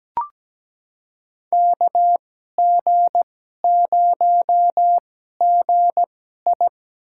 Morse code, heard as a keyed tone, about 17 wpm, spaced as for 12 wpm, 700 Hz. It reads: KG0GI
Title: USA Call Signs - 5 Characters - 2x Character Spacing - 17wpm